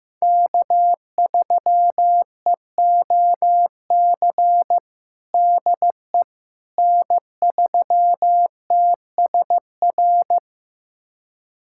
Morse code, heard as a keyed tone, about 15 wpm, 700 Hz. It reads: K3EOC DE N3TSR
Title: USA Callsign DE Callsign - 5 Characters - 15wpm